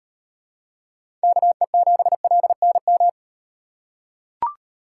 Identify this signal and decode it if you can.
Morse code, heard as a keyed tone, about 36 wpm, 700 Hz. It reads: KE7LNM